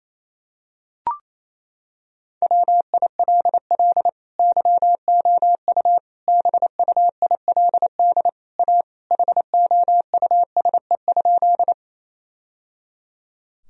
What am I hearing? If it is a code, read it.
Morse code, heard as a keyed tone, about 28 wpm, 700 Hz. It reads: WILL YOU BUILD A HOUSE?
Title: Sentences from Top 1000 Words - Review Part 1 - 28wpm